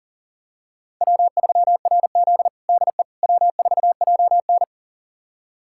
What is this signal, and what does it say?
Morse code, heard as a keyed tone, about 40 wpm, 700 Hz. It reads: W3RZ DE W4JN
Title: USA Callsign DE Callsign - 4 Characters - 40wpm